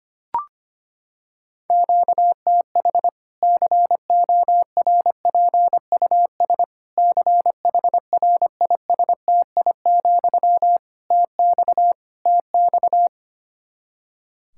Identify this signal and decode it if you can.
Morse code, heard as a keyed tone, about 25 wpm, 700 Hz. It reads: QTH CORPUS CHRISTI, TX TX
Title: Practice QSO - Exchange 1 - Beginning - 25wpm